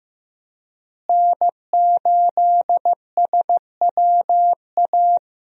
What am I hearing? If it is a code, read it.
Morse code, heard as a keyed tone, about 15 wpm, 700 Hz. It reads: N8SWA
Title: USA Call Signs - 5 Characters - Repeated 2x - 15wpm